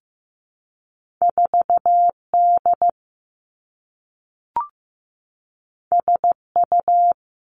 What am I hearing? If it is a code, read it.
Morse code, heard as a keyed tone, about 15 wpm, 700 Hz. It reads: 4D SU
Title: Two Challenging Letter-Number Combinations 15wpm